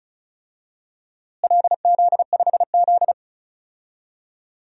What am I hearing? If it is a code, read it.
Morse code, heard as a keyed tone, about 36 wpm, 700 Hz. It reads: LZ5Z
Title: DXCC-100 Call Signs - 4 Characters - Repeated 3x - 35wpm